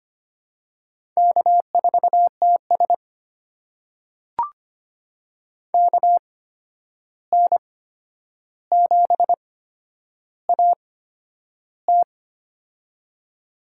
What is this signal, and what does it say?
Morse code, heard as a keyed tone, about 25 wpm, spaced as for 10 wpm, 700 Hz. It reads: K4TS KN7AT
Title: USA Call Signs - 8x Character Spacing - 25wpm